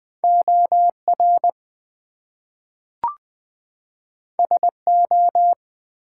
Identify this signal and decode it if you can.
Morse code, heard as a keyed tone, about 20 wpm, 700 Hz. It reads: OR SO
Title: Two Letter Words - 20wpm